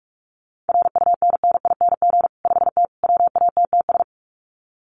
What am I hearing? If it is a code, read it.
Morse code, heard as a keyed tone, about 44 wpm, 700 Hz. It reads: RUNNING 5T WATTS
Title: Common QSO Phrases 45wpm